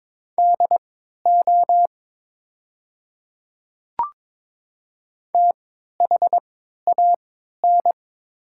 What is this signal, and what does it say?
Morse code, heard as a keyed tone, about 21 wpm, spaced as for 12 wpm, 700 Hz. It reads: DO THAN
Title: 3x Character Spacing - Top 100 Words - 22wpm